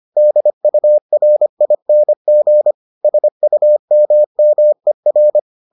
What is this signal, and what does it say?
Morse code, heard as a keyed tone, about 25 wpm, 600 Hz. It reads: DURING SUMMER